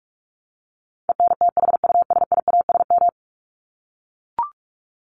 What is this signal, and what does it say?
Morse code, heard as a keyed tone, about 44 wpm, 700 Hz. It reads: ENTHUSIASM
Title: Top 1751-2000 Words 45wpm